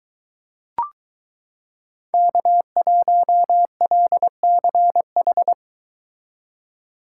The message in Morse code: K1LCH

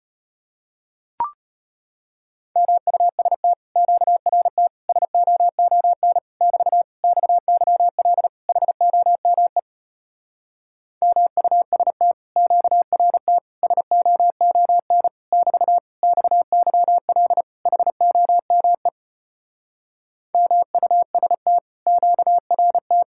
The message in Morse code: MUST QRT SOON <BT> XYL HOME MUST QRT SOON <BT> XYL HOME MUST QRT

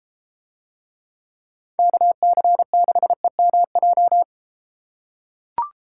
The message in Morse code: KC6EMJ

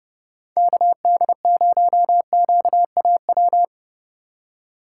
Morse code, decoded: KD0QAW

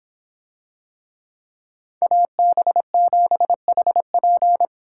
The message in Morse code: AB7HP

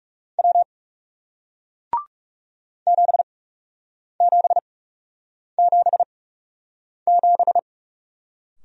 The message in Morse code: W 7777

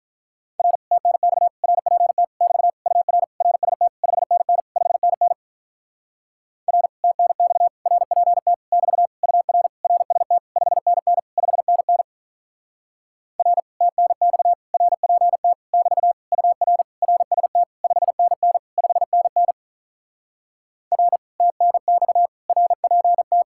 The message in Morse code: R TNX RPT <BT> UR RST 5NN 5NN R TNX RPT <BT> UR RST 5NN 5NN R TNX RPT <BT> UR RST 5NN 5NN R TNX RPT

